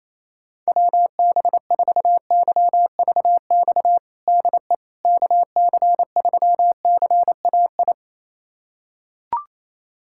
Morse code, heard as a keyed tone, about 28 wpm, 700 Hz. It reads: WB4YVX DE KC3CAI
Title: USA Callsign DE Callsign - 6 Characters - 28wpm